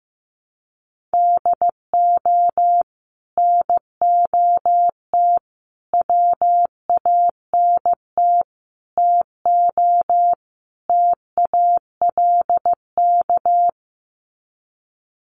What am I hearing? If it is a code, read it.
Morse code, heard as a keyed tone, about 15 wpm, 700 Hz. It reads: DO NOT WANT TO TALK
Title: Sets of 5 Words 15wpm